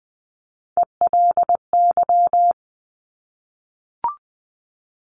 Morse code, heard as a keyed tone, about 20 wpm, 700 Hz. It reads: ELY
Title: Top Three Letter Combinations in Words 20wpm